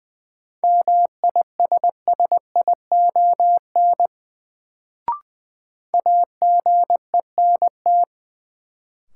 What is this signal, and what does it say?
Morse code, heard as a keyed tone, about 20 wpm, 700 Hz. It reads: MISSION AGENT
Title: Sentences from Top 1600 Words - New Word Drill - 20wpm